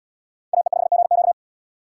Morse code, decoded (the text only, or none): AE5GQ